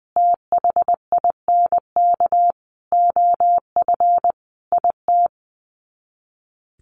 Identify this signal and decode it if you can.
Morse code, heard as a keyed tone, about 20 wpm, 700 Hz. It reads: THINK OF IT
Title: Sets of 5 Words Encore 20wpm